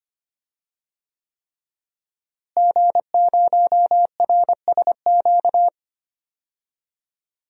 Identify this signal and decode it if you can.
Morse code, heard as a keyed tone, about 25 wpm, 700 Hz. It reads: G0RSQ
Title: DXCC-100 Call Signs - 5 Characters - Repeated 3x - 25wpm